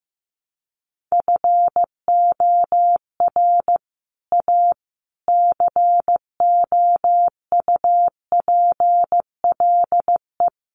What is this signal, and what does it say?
Morse code, heard as a keyed tone, about 15 wpm, 700 Hz. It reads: FOR A COUPLE
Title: Sets of 4 Words 15wpm